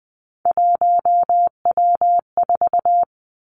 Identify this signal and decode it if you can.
Morse code, heard as a keyed tone, about 20 wpm, 700 Hz. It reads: 1W4
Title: Three Challenging Letter-Number Combinations 20wpm